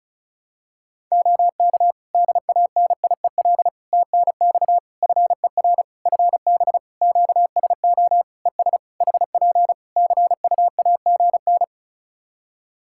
Morse code, heard as a keyed tone, about 34 wpm, 700 Hz. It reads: OK DANIEL TNX FER FB QSO ES HP CUAGN
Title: Practice QSO - Exchange 4 - Ending - 35wpm